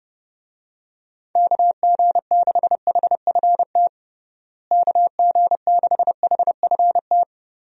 Morse code, heard as a keyed tone, about 30 wpm, 700 Hz. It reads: KG6HFT KG6HFT
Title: USA Call Signs - 6 Characters - Repeated 3x - 30wpm